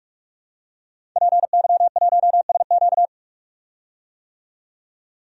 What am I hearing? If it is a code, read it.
Morse code, heard as a keyed tone, about 46 wpm, 700 Hz. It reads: PY1SQ
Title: DXCC-100 Call Signs - 5 Characters - 45wpm